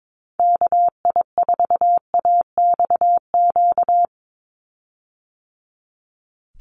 Morse code, heard as a keyed tone, about 22 wpm, 700 Hz. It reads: KI4AXQ
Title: USA Call Signs 22wpm